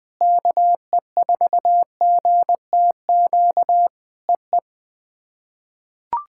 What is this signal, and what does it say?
Morse code, heard as a keyed tone, about 21 wpm, 700 Hz. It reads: KE4GTQ EE
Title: POTA QSO Practice - 20wpm